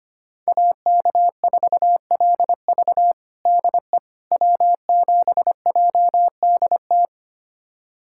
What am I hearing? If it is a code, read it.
Morse code, heard as a keyed tone, about 25 wpm, 700 Hz. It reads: AK4LV DE W7JDT
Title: USA Callsign DE Callsign - 5 Characters - 25wpm